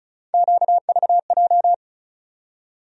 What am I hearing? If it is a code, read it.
Morse code, heard as a keyed tone, about 34 wpm, 700 Hz. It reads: QVJ
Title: USA Call Sign Postfixes - 3 Letters 35wpm